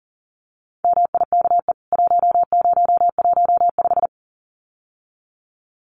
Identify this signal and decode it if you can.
Morse code, heard as a keyed tone, about 40 wpm, 700 Hz. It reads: MIKE 1015
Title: CWT Exchange 40wpm